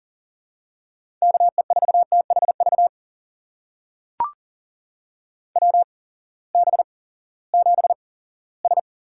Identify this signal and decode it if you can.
Morse code, heard as a keyed tone, about 40 wpm, spaced as for 20 wpm, 700 Hz. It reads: KE4THV WB7S
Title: USA Call Signs - 6 Characters - 8x Character Spacing - 40wpm